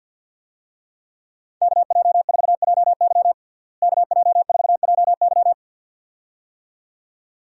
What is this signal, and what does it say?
Morse code, heard as a keyed tone, about 52 wpm, 700 Hz. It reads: KJ4JY KJ4JY
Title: USA Call Signs - 5 Characters - Repeated 2x - 50wpm